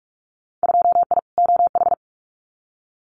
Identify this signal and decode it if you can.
Morse code, heard as a keyed tone, about 44 wpm, 700 Hz. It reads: 2I OH
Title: Winter Field Day Exchanges 45wpm